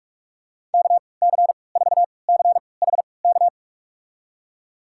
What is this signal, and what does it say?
Morse code, heard as a keyed tone, about 46 wpm, spaced as for 26 wpm, 700 Hz. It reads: KC4CHK
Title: USA Call Signs - 3x Character Spacing - 45wpm